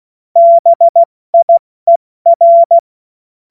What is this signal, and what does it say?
Morse code, heard as a keyed tone, about 16 wpm, 680 Hz. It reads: BIER
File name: Morsekurs / Witze / Witz 27. Juli 2021, 70 F80